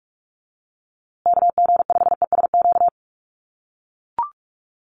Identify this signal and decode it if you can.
Morse code, heard as a keyed tone, about 44 wpm, 700 Hz. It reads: KG5ESQ
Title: USA Call Signs Repeated 3x 45wpm